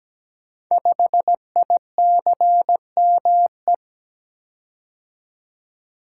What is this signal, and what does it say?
Morse code, heard as a keyed tone, about 17 wpm, 700 Hz. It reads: 5ICME
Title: Code Groups - 5 Letter:Numbers - 17wpm